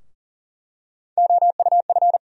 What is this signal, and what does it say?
Morse code, heard as a keyed tone, about 40 wpm, 700 Hz. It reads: OUF
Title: USA Call Sign Postfixes - 3 Letters 40wpm